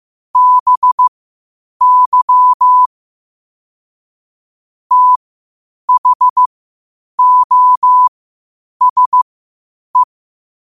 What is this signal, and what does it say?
Morse code, heard as a keyed tone, about 15 wpm, spaced as for 8 wpm, 1000 Hz. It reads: BY THOSE
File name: qotdfiles / 8wpm